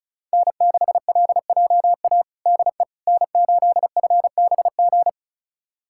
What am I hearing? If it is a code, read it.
Morse code, heard as a keyed tone, about 36 wpm, 700 Hz. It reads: N6LJA DE N8FBG
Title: USA Callsign DE Callsign - 5 Characters - 35wpm